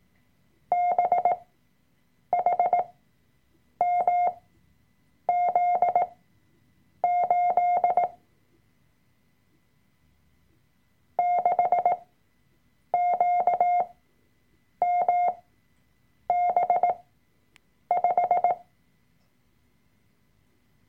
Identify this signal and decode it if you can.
Morse code, heard as a keyed tone, about 18 wpm, spaced as for 7 wpm, 710 Hz. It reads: BHMZ8 6QMB5
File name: Workshop Dateien / CW-Online-Kurs / Lektion 003